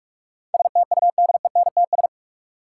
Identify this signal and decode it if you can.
Morse code, heard as a keyed tone, about 42 wpm, 700 Hz. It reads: STUDENTS
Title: Sets of 5 Words Encore - New Word Drill - 45wpm